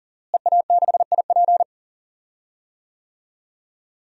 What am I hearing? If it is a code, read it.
Morse code, heard as a keyed tone, about 38 wpm, 700 Hz. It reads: EA6IP
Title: DXCC-100 Call Signs - 5 Characters - Repeated 3x - 40wpm